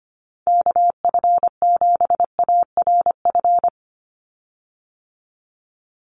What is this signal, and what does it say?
Morse code, heard as a keyed tone, about 25 wpm, 700 Hz. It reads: KF7ARF
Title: USA Call Signs 25wpm